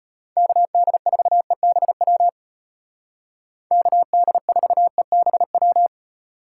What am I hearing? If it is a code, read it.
Morse code, heard as a keyed tone, about 32 wpm, 700 Hz. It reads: KD4EBW KD4EBW